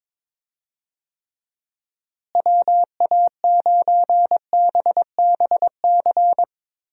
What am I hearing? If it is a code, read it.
Morse code, heard as a keyed tone, about 22 wpm, 700 Hz. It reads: WA9BBC
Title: USA Call Signs - 6 Characters - Repeated 2x - 22wpm